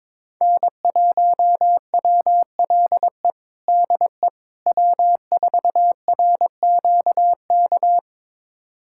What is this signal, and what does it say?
Morse code, heard as a keyed tone, about 21 wpm, 700 Hz. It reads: N1WLE DE W4RQK
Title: USA Callsign DE Callsign - 5 Characters - 22wpm